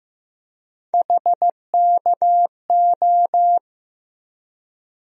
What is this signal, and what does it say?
Morse code, heard as a keyed tone, about 15 wpm, 700 Hz. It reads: HKO
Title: USA Call Sign Postfixes - 3 Letters 15wpm